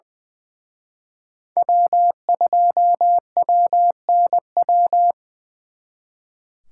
Morse code, heard as a keyed tone, about 23 wpm, 700 Hz. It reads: W2WNW
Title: Speed Racing - USA 1x3 Call Signs 20wpm